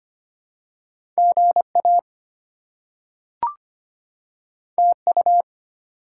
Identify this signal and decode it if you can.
Morse code, heard as a keyed tone, about 25 wpm, 700 Hz. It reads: GA TU
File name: Top 20 QSO Elements - 25wpm